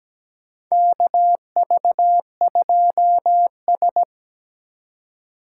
Morse code, heard as a keyed tone, about 17 wpm, 700 Hz. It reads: KV2S
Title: USA Call Signs - 4 Characters - 3x Character Spacing - 17wpm